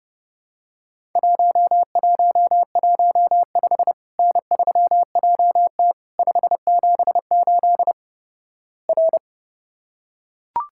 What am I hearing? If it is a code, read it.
Morse code, heard as a keyed tone, about 30 wpm, 700 Hz. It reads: 1115 N3JT 578 R
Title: Worked All Europe DX Contest - QTC Exchange - 30wpm